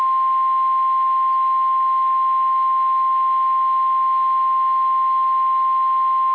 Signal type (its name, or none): Emergency Alert System (SAME) tones